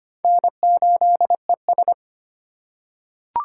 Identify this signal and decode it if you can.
Morse code, heard as a keyed tone, about 25 wpm, 700 Hz. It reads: N8ES
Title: USA Call Signs - 4 Characters - 2x Character Spacing - 25wpm